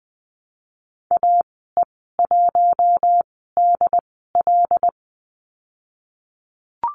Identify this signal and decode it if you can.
Morse code, heard as a keyed tone, about 20 wpm, spaced as for 15 wpm, 700 Hz. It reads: AE1DL
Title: 082 Lesson 14g - Number 2 - Call Signs 20wpm